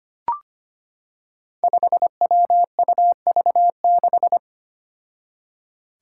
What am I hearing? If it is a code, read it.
Morse code, heard as a keyed tone, about 25 wpm, 700 Hz. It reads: 5WUV6
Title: Five Challenging Letter-Number Combinations 25wpm